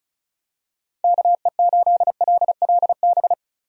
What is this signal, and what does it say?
Morse code, heard as a keyed tone, about 36 wpm, 700 Hz. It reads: KE8LLB
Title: USA Call Signs - 6 Characters - Repeated 2x - 35wpm